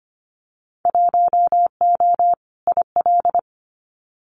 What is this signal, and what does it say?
Morse code, heard as a keyed tone, about 25 wpm, 700 Hz. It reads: 1O IL
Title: Winter Field Day Exchanges 25wpm